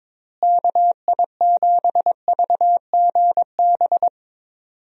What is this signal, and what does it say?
Morse code, heard as a keyed tone, about 22 wpm, 700 Hz. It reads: KI7VGB
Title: USA Call Signs - 6 Characters - Repeated 2x - 22wpm